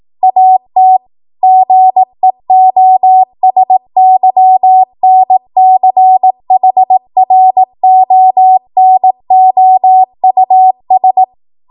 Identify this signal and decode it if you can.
Morse code, heard as a keyed tone, about 18 wpm, 750 Hz. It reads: AT GEOSYNCHRONOUS
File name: Morse / 18WPM